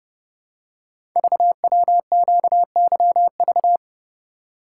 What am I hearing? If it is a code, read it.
Morse code, heard as a keyed tone, about 30 wpm, 700 Hz. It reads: VWQYV
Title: Code Groups - 5 Letters - 30wpm